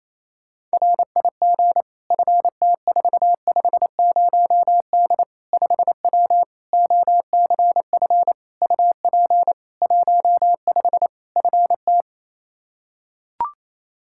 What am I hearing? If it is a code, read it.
Morse code, heard as a keyed tone, about 28 wpm, 700 Hz. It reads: RIG FT450D 5W OCF UP 15 FT
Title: K1USN SST QSO Practice 28wpm